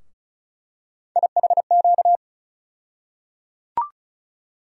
Morse code, heard as a keyed tone, about 34 wpm, 700 Hz. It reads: IHQ
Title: USA Call Sign Postfixes - 3 Letters 35wpm